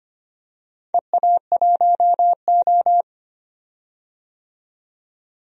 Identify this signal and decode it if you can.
Morse code, heard as a keyed tone, about 25 wpm, 700 Hz. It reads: EA1O
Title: DXCC-100 Call Signs - 4 Characters - 25wpm